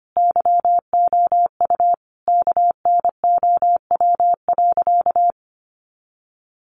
Morse code, heard as a keyed tone, about 25 wpm, 700 Hz. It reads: YOU KNOW.
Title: Sentences from Top 1500 Words 25wpm